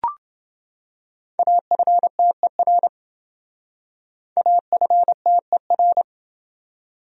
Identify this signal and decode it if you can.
Morse code, heard as a keyed tone, about 29 wpm, 700 Hz. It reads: AFTER AFTER